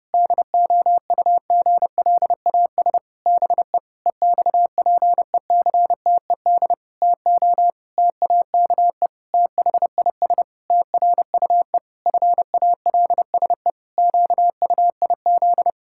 Morse code, heard as a keyed tone, about 30 wpm, 700 Hz. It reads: DOUGLAS BE EXPECTED TO TAKE THIS TRUE FALSE QUIZ
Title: Pangram Challenge v2 - 30wpm